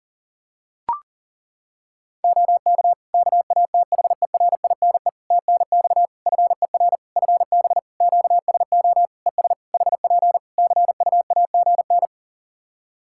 Morse code, heard as a keyed tone, about 40 wpm, 700 Hz. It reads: OK KATHERINE TNX FER FB QSO ES HP CUAGN
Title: K1USN SST QSO Practice 40wpm